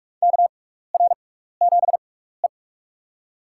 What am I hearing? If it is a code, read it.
Morse code, heard as a keyed tone, about 42 wpm, spaced as for 15 wpm, 700 Hz. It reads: KR7E
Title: USA Call Signs - 4 Characters - 6x Character Spacing - 45wpm